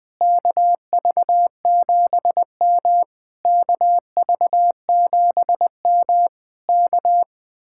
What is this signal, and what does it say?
Morse code, heard as a keyed tone, about 20 wpm, 700 Hz. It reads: KV7M KV7M K